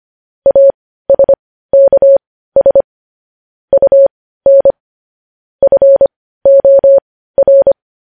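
Morse code, heard as a keyed tone, about 25 wpm, spaced as for 15 wpm, 550 Hz. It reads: ASKS UN FOR
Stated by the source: Morse news headlines practice at 15 wpm.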